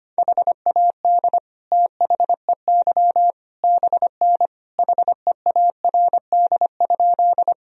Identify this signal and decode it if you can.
Morse code, heard as a keyed tone, about 25 wpm, 700 Hz. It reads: HAD THEY BN HEARD?
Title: Sentences from Top 400 Words and Abbreviations 25wpm